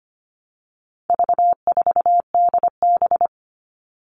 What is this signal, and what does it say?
Morse code, heard as a keyed tone, about 25 wpm, 700 Hz. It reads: V4DB
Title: All Cut Numbers - 4-Digit 25wpm